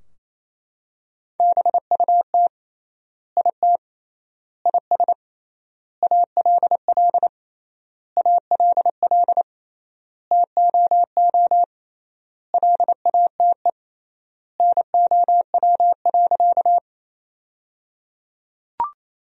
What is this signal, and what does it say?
Morse code, heard as a keyed tone, about 28 wpm, 700 Hz. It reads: BUT IT IS ALL ALL TOO LATE NOW.
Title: 3x Word-Spacing - Sentences from Top 300 Words - 28wpm